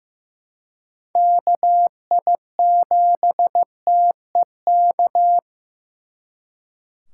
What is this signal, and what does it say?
Morse code, heard as a keyed tone, about 18 wpm, 700 Hz. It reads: KI7TEK
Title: Speed Racing - USA 2x3 Call Signs 15wpm